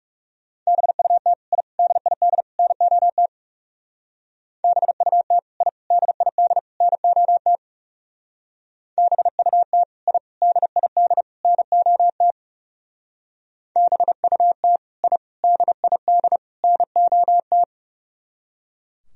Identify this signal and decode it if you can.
Morse code, heard as a keyed tone, about 36 wpm, 700 Hz. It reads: BUT I DID NOT BUT I DID NOT BUT I DID NOT BUT I DID NOT
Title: Speed-Racing - Sets of 4 Words 30wpm